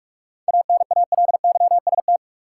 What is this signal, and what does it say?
Morse code, heard as a keyed tone, about 41 wpm, 700 Hz. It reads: ANALYST
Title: Practice QSO - Exchange 3 - More Info - 45wpm